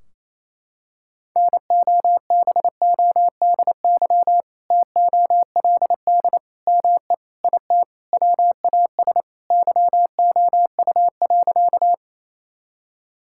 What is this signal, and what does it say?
Morse code, heard as a keyed tone, about 27 wpm, 700 Hz. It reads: NOBODY TOLD ME IT WAS YOU.
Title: Sentences from Top 1600 Words 28wpm